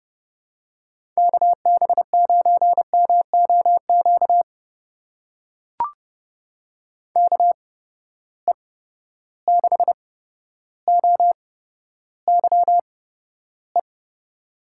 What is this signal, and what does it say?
Morse code, heard as a keyed tone, about 31 wpm, spaced as for 13 wpm, 700 Hz. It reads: KB9MOQ KE6OYE
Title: USA Call Signs - 6 Characters - 8x Character Spacing - 30wpm